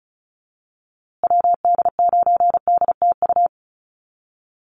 Morse code, heard as a keyed tone, about 34 wpm, 700 Hz. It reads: WD9DTU
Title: USA Call Signs Repeated 3x 35wpm